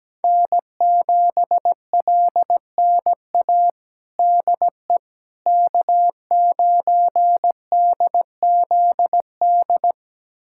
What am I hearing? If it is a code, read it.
Morse code, heard as a keyed tone, about 17 wpm, 700 Hz. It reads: N7LNA DE K9DZD
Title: USA Callsign DE Callsign - 5 Characters - 17wpm